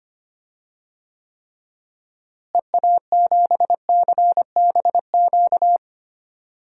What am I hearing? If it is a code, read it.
Morse code, heard as a keyed tone, about 25 wpm, 700 Hz. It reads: EA7CBQ
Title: DXCC-100 Call Signs - 6 Characters - Repeated 3x - 25wpm